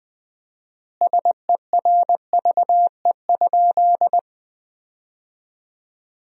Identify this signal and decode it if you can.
Morse code, heard as a keyed tone, about 20 wpm, 700 Hz. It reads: SERVE?
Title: 4x Word-Spacing - Sentences from Top 400 Words - 20wpm